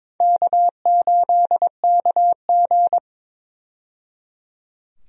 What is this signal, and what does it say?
Morse code, heard as a keyed tone, about 22 wpm, 700 Hz. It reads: K8KG
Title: USA Call Signs - 4 Characters - 22wpm